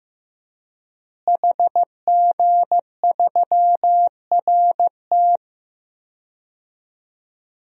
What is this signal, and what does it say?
Morse code, heard as a keyed tone, about 15 wpm, 700 Hz. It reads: HG3RT
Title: DXCC-100 Call Signs - 5 Characters - Repeated 3x - 15wpm